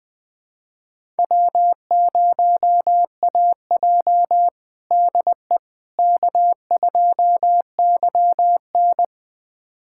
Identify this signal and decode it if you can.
Morse code, heard as a keyed tone, about 21 wpm, 700 Hz. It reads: W0AJ DE K2YN
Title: USA Callsign DE Callsign - 4 Characters - 20wpm